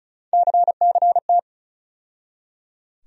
Morse code, heard as a keyed tone, about 36 wpm, 700 Hz. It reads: CCT
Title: USA Call Sign Postfixes - 3 Letters 35wpm